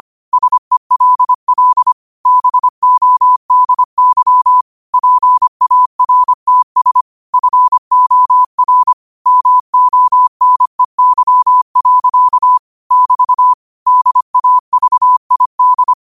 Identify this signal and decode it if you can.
Morse code, heard as a keyed tone, about 25 wpm, 1000 Hz. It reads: SELL BODY PARTS FOR MONEY. <BT> DAVID
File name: qotdfiles / 25wpm